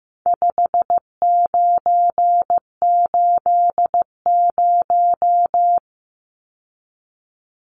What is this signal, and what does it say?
Morse code, heard as a keyed tone, about 15 wpm, 700 Hz. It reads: 5980
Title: Numbers - 4-Digit 15wpm